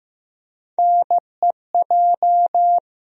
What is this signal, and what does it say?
Morse code, heard as a keyed tone, about 15 wpm, 700 Hz. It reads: NEJ
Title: USA Call Sign Postfixes - 3 Letters 15wpm